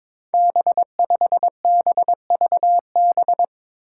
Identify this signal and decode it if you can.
Morse code, heard as a keyed tone, about 22 wpm, 700 Hz. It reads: B5BVB
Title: Five Challenging Letter-Number Combinations 22wpm